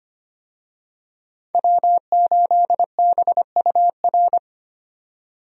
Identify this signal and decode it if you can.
Morse code, heard as a keyed tone, about 25 wpm, 700 Hz. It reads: W8BUR
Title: USA Call Signs - 5 Characters - 25wpm